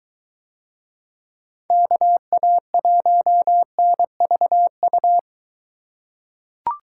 Morse code, heard as a keyed tone, about 21 wpm, 700 Hz. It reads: KA1NVU